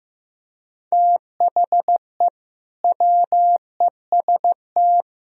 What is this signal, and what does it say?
Morse code, heard as a keyed tone, about 15 wpm, 700 Hz. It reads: THE WEST
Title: Speed-Racing - Sets of 5 Words 15wpm